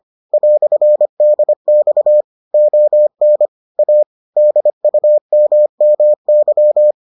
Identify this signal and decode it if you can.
Morse code, heard as a keyed tone, about 25 wpm, 600 Hz. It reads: "DX ON A DUMMY